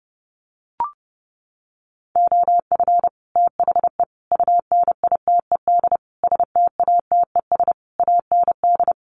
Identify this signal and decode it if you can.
Morse code, heard as a keyed tone, about 30 wpm, 700 Hz. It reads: OF THE UNITED STATES AND
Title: Sets of 5 Words 30wpm